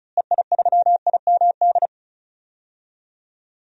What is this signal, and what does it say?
Morse code, heard as a keyed tone, about 36 wpm, 700 Hz. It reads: EI3IMD
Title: DXCC-100 Call Signs - 6 Characters - Repeated 2x - 35wpm